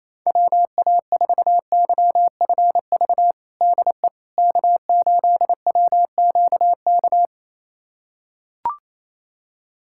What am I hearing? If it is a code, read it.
Morse code, heard as a keyed tone, about 28 wpm, 700 Hz. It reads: WA4YFV DE K8WQK
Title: USA Callsign DE Callsign 28wpm